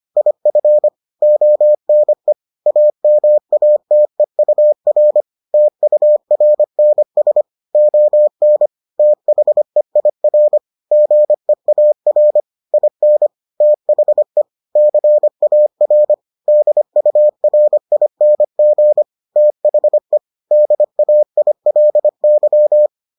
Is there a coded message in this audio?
Yes: Morse code from 0.2 to 22.9 s, heard as a keyed tone, about 25 wpm, 600 Hz. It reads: IF ONE AMATEUR TURNS ON THEIR GEAR IN THE CAR DURING THE DAILY